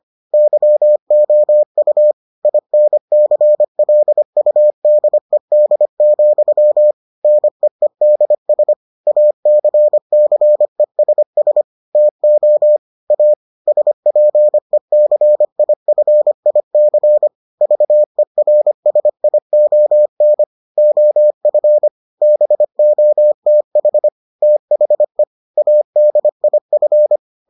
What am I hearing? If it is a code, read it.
Morse code, heard as a keyed tone, about 25 wpm, 600 Hz. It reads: YOU INCLUDED, NEEDS ACCESS TO A SPECIFIC VERSION OF BOTH THE ADIF